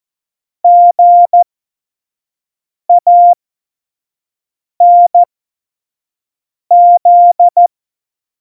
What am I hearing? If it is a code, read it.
Morse code, heard as a keyed tone, about 14 wpm, spaced as for 5 wpm, 700 Hz. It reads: GANZ
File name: Morsekurs / Witze / Witz 15. Dezember, F70